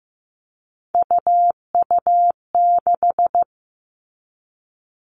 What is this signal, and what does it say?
Morse code, heard as a keyed tone, about 15 wpm, 700 Hz. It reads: UU6
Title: All Cut Numbers - 3-Digit 15wpm